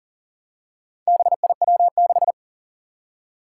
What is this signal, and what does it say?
Morse code, heard as a keyed tone, about 41 wpm, 700 Hz. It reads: BIW6
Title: Four Challenging Letter-Number Combinations 40wpm